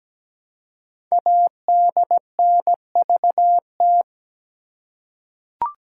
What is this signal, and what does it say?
Morse code, heard as a keyed tone, about 17 wpm, 700 Hz. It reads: ADNVT
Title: Code Groups - 5 Letters - 17wpm